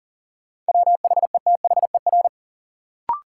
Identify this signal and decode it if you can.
Morse code, heard as a keyed tone, about 40 wpm, 700 Hz. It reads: WHETHER
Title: Sentences from Top 1000 Words - Review Part 3 - New Word Drill - 40wpm